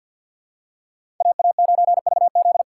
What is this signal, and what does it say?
Morse code, heard as a keyed tone, about 50 wpm, 700 Hz. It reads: AA9VZ
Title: USA Call Signs - 6x Character Spacing - 50wpm